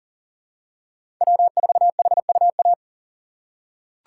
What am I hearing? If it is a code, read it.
Morse code, heard as a keyed tone, about 40 wpm, 700 Hz. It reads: W4HUA
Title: USA Call Signs - 5 Characters - 6x Character Spacing - 40wpm